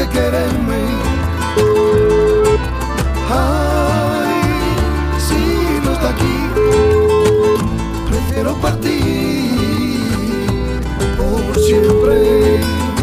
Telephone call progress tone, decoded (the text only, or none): ringback tone